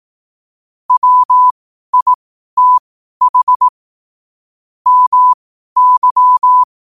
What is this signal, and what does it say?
Morse code, heard as a keyed tone, about 18 wpm, spaced as for 12 wpm, 1000 Hz. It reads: WITH MY
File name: qotdfiles / 12wpm